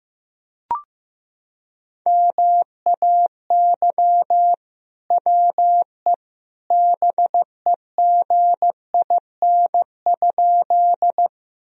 Morse code, heard as a keyed tone, about 15 wpm, 700 Hz. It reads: MAY WE BEGIN?
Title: Sentences from Top 1000 Words - Review Part 1 - 15wpm